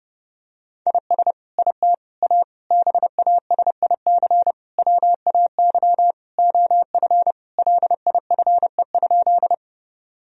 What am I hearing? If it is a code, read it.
Morse code, heard as a keyed tone, about 30 wpm, 700 Hz. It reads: IS IT A BASIC WAY OF LIFE?
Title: Sentences from Top 1000 Words - Review Part 3 - 30wpm